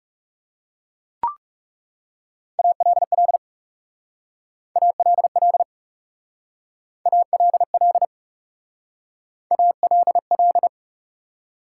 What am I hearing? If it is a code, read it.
Morse code, heard as a keyed tone, about 38 wpm, 700 Hz. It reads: ALL ALL ALL ALL